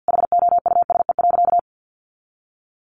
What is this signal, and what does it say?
Morse code, heard as a keyed tone, about 50 wpm, 700 Hz. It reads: HOUSE.
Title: Sentences from Top 1400 Words 50wpm